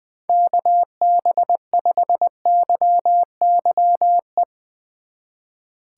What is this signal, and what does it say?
Morse code, heard as a keyed tone, about 21 wpm, 700 Hz. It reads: KB5YYE
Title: Speed Racing - USA Call Signs 20wpm